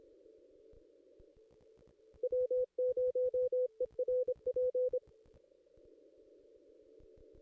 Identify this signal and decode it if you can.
Morse code, heard as a keyed tone, about 26 wpm, 500 Hz. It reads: W0ERP
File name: contests / ss-cw-k5tr-op-n5rz -1700